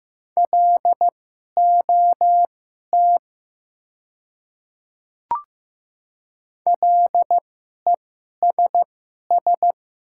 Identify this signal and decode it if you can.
Morse code, heard as a keyed tone, about 15 wpm, spaced as for 11 wpm, 690 Hz. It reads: LOT LESS
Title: 2x Character Spacing - Top 500 Words - 15wpm